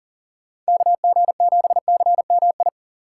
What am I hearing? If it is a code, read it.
Morse code, heard as a keyed tone, about 40 wpm, 700 Hz. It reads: KG7CMI
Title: USA Call Signs - 3x Character Spacing - 40wpm